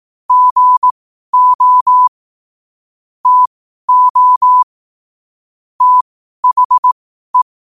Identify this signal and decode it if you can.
Morse code, heard as a keyed tone, about 18 wpm, spaced as for 12 wpm, 1000 Hz. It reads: GO TO THE